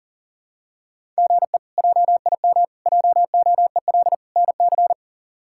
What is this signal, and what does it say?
Morse code, heard as a keyed tone, about 40 wpm, 690 Hz. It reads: GE JIM JOEL NC
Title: K1USN SST QSO Practice - 40wpm